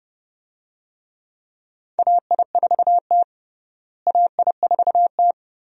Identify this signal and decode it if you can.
Morse code, heard as a keyed tone, about 32 wpm, 700 Hz. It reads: AI4T AI4T
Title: USA Call Signs - 4 Characters - Repeated 2x - 30wpm